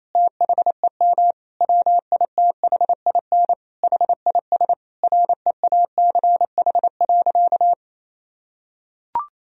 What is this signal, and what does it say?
Morse code, heard as a keyed tone, about 28 wpm, 700 Hz. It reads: THEM WITHIN HIS REACH.
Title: Sentences from Top 1000 Words - Review Part 4 - 28wpm